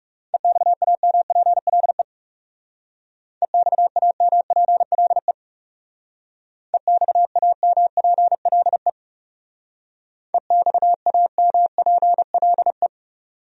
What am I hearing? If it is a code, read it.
Morse code, heard as a keyed tone, about 39 wpm, 700 Hz. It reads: EXAMPLE EXAMPLE EXAMPLE EXAMPLE